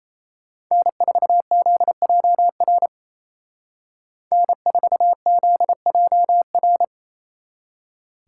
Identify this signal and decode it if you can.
Morse code, heard as a keyed tone, about 33 wpm, 700 Hz. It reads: N4ZJR N4ZJR
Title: Speed Racing - USA 1x3 Call Signs 28wpm